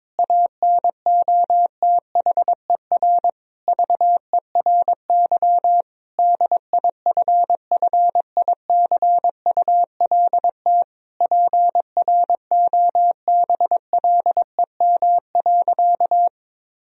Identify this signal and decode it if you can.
Morse code, heard as a keyed tone, about 22 wpm, 700 Hz. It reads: ANOTHER VERY DIFFICULT PROBLEM.
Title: Sentences from Top 1000 Words - Review Part 2 - 22wpm